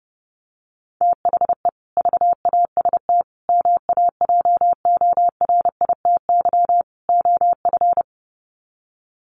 Morse code, heard as a keyed tone, about 30 wpm, 700 Hz. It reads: THE VAST MAJORITY OF
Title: Sets of 4 Words 30wpm